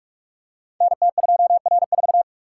Morse code, heard as a keyed tone, about 44 wpm, 690 Hz. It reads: NT2R4
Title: Code Groups - 5 Letter:Numbers - 45wpm